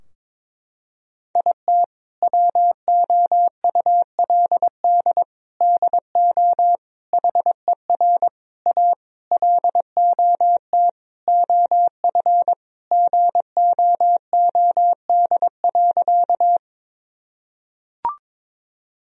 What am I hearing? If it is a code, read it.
Morse code, heard as a keyed tone, about 22 wpm, 700 Hz. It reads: IT WOULD DO HER A LOT OF GOOD.
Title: Sentences from Top 1000 Words - Review Part 3 - 22wpm